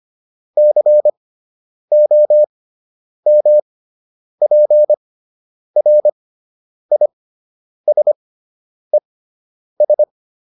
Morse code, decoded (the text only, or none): COMPRISES